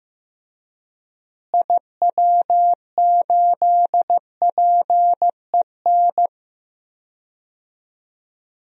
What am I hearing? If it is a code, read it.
Morse code, heard as a keyed tone, about 15 wpm, 700 Hz. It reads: IW8PEN
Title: DXCC-100 Call Signs - 6 Characters - Repeated 2x - 15wpm